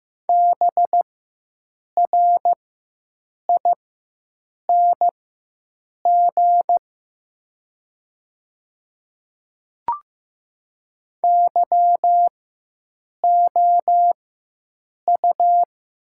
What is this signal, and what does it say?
Morse code, heard as a keyed tone, about 15 wpm, spaced as for 7 wpm, 700 Hz. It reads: BRING YOU
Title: 4x Character Spacing - Top 500 Words - 15wpm